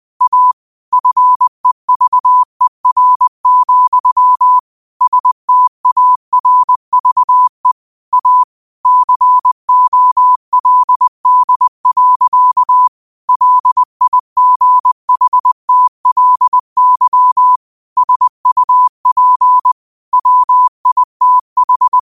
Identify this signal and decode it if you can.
Morse code, heard as a keyed tone, about 20 wpm, 1000 Hz. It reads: A FEVER, STARVE A COLD. LIGHTLY SUP WITH
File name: qotdfiles / 20wpm